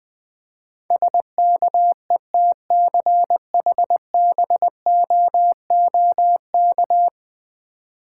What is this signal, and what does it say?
Morse code, heard as a keyed tone, about 20 wpm, 700 Hz. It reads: SKETCHBOOK
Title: Difficult Words with Hints - 20wpm